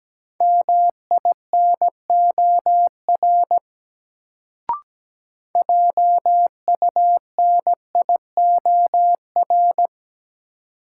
Morse code, heard as a keyed tone, about 17 wpm, 700 Hz. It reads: MINOR JUNIOR
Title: Sentences from Top 1500 Words - New Word Drill Alt - 17wpm